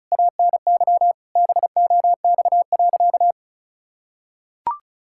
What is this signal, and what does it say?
Morse code, heard as a keyed tone, about 36 wpm, 700 Hz. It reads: ANY BOX.
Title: Sentences from Top 1000 Words - Review Part 5 - 35wpm